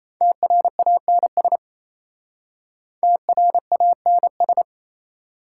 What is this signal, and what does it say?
Morse code, heard as a keyed tone, about 35 wpm, 690 Hz. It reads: TRANS TRANS